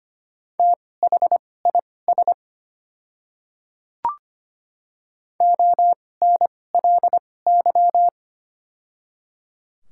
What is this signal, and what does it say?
Morse code, heard as a keyed tone, about 25 wpm, spaced as for 18 wpm, 690 Hz. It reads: THIS ONLY
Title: 2x Character Spacing - Top 200 Words - 25wpm